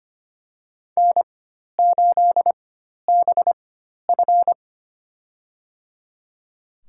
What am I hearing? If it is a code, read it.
Morse code, heard as a keyed tone, about 25 wpm, spaced as for 12 wpm, 700 Hz. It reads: N8BF
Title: USA Call Signs - 4 Characters - 4x Character Spacing - 25wpm